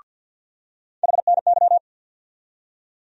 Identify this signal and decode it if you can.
Morse code, heard as a keyed tone, about 51 wpm, 700 Hz. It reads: HNY
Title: USA Call Sign Postfixes - 3 Letters 50wpm